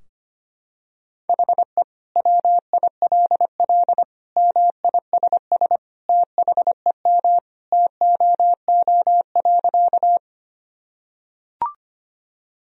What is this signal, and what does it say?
Morse code, heard as a keyed tone, about 25 wpm, 700 Hz. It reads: HE WILL MISS THEM TOO.
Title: Sentences from Top 1000 Words - Review Part 3 - 25wpm